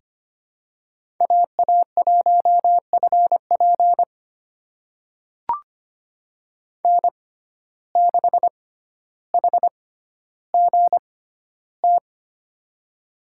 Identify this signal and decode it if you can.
Morse code, heard as a keyed tone, about 25 wpm, spaced as for 13 wpm, 700 Hz. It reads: AA1FP N6HGT
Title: USA Call Signs - 5 Characters - 6x Character Spacing - 25wpm